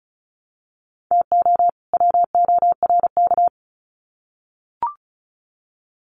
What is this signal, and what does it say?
Morse code, heard as a keyed tone, about 34 wpm, 700 Hz. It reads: TO WORK
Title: Sets of 2 Words 35wpm